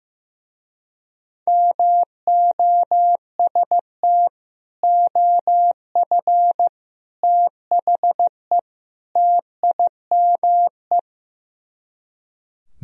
Morse code, heard as a keyed tone, about 15 wpm, 700 Hz. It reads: MOST OF THE TIME
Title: Speed-Racing - Sets of 4 Words 15wpm